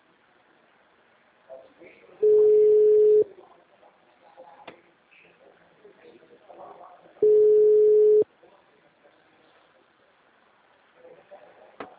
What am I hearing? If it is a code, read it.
Ringback tone (Europe, 425 Hz), with speech.